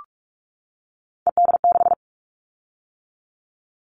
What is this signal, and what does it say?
Morse code, heard as a keyed tone, about 46 wpm, 700 Hz. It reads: ED6
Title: Callsign Prefixes for Top 50 DXCC - 45wpm